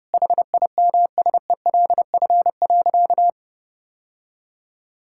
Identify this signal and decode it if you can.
Morse code, heard as a keyed tone, about 32 wpm, 700 Hz. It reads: HIMSELF.
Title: Sentences from Top 1600 Words 30wpm